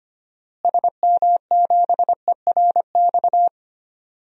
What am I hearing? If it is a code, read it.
Morse code, heard as a keyed tone, about 25 wpm, 700 Hz. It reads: SM7ERX
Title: DXCC-100 Call Signs - 6 Characters - Repeated 3x - 25wpm